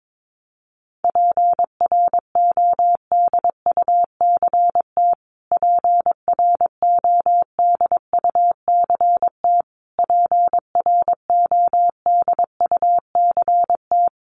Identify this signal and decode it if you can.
Morse code, heard as a keyed tone, about 22 wpm, 700 Hz. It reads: PRODUCT PRODUCT PRODUCT
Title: Top 500 Words Repeated 22wpm